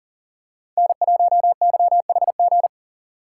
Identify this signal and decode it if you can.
Morse code, heard as a keyed tone, about 40 wpm, 700 Hz. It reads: N1YHG
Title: USA Call Signs - 5 Characters - 4x Character Spacing - 40wpm